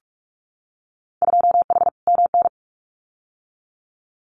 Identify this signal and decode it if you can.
Morse code, heard as a keyed tone, about 46 wpm, 700 Hz. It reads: 2H MN
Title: Winter Field Day Exchanges 45wpm